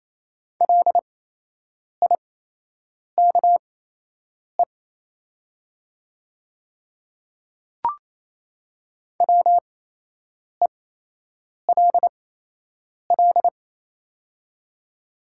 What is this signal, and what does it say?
Morse code, heard as a keyed tone, about 28 wpm, spaced as for 8 wpm, 700 Hz. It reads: LIKE WELL
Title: 8x Character Spacing - Top 300 Words - 28wpm